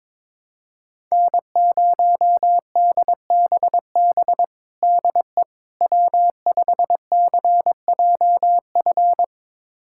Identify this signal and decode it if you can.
Morse code, heard as a keyed tone, about 22 wpm, 700 Hz. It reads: N0DBB DE W5CJF
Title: USA Callsign DE Callsign - 5 Characters - 22wpm